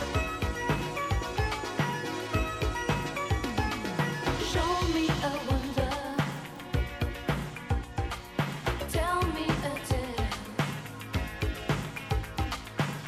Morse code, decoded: none